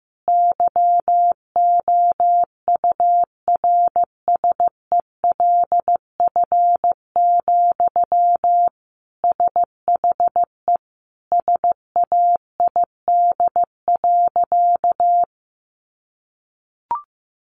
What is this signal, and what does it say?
Morse code, heard as a keyed tone, about 15 wpm, 700 Hz. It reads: YOURSELF, SHE SAID.
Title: Sentences from Top 600 Words 15wpm